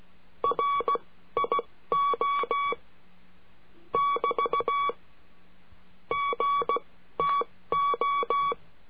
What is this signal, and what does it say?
Morse code, heard as a keyed tone, about 16 wpm, spaced as for 13 wpm, 1130 Hz. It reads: RIO <BT> GTO
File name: CURSO DE TELEGRAFIA